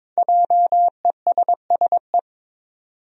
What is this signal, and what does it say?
Morse code, heard as a keyed tone, about 21 wpm, 700 Hz. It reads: JESSE
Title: USA Common Names - 22wpm